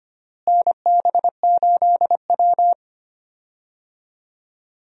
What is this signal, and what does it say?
Morse code, heard as a keyed tone, about 25 wpm, 700 Hz. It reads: NB8W
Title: USA Call Signs - 4 Characters - Repeated 2x - 25wpm